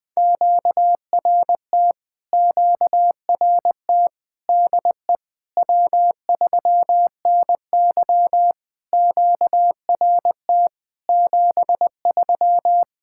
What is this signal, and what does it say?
Morse code, heard as a keyed tone, about 20 wpm, 700 Hz. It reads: QRT QRT DE W3NY QRT 73
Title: POTA QSO Practice - 20wpm